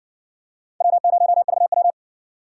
Morse code, heard as a keyed tone, about 60 wpm, 700 Hz. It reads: W0VW